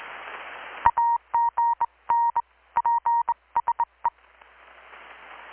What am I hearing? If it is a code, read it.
Morse code, heard as a keyed tone, about 21 wpm, 940 Hz. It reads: AGN PSE
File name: blog / swl / 6-abril-2019-12695-USB-KPS